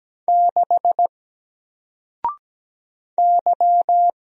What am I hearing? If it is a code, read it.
Morse code, heard as a keyed tone, about 17 wpm, 700 Hz. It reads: 6 Y
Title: Single Letter-Number 17wpm